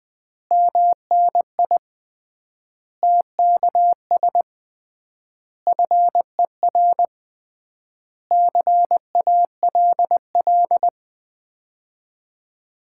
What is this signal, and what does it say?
Morse code, heard as a keyed tone, about 20 wpm, 700 Hz. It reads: MNI TKS FER CALL
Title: Common QSO Phrases - 3x Word-Spacing - 20wpm